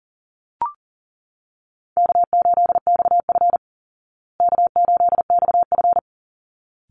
Morse code, heard as a keyed tone, about 40 wpm, 700 Hz. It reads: K8XF K8XF
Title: Call Signs Repeated 3x - 40wpm